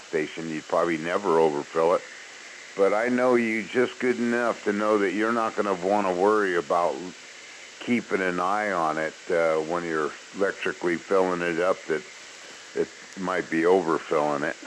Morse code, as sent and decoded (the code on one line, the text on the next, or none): none